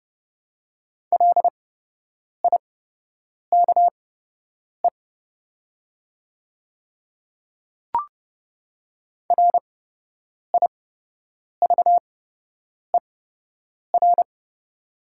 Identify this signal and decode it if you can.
Morse code, heard as a keyed tone, about 30 wpm, spaced as for 8 wpm, 700 Hz. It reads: LIKE RIVER
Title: 8x Character Spacing - Top 300 Words - 30wpm